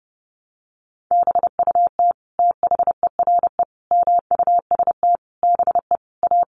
Morse code, heard as a keyed tone, about 30 wpm, 700 Hz. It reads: BUT THERE MUST BE A
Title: Sentences from Top 1500 Words 30wpm